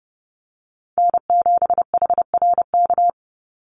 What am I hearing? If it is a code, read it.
Morse code, heard as a keyed tone, about 30 wpm, 700 Hz. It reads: N7HRK
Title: Call Signs Repeated 3x - 30wpm